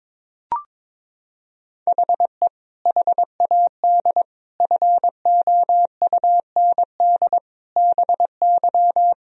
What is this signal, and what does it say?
Morse code, heard as a keyed tone, about 22 wpm, 700 Hz. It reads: HE HAD FOUND BY
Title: Sentences from Top 1000 Words - Review Part 3 - 22wpm